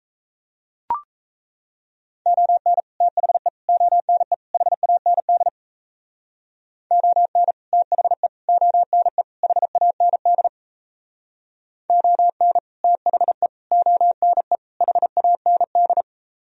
Morse code, heard as a keyed tone, about 37 wpm, 700 Hz. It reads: ON THE ONE HAND ON THE ONE HAND ON THE ONE HAND